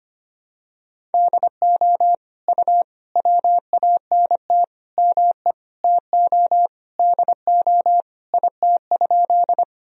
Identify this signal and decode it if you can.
Morse code, heard as a keyed tone, about 25 wpm, 700 Hz. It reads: DO U WANT ME TO DO IT?
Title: Sentences from Top 200 Words and Abbreviations 25wpm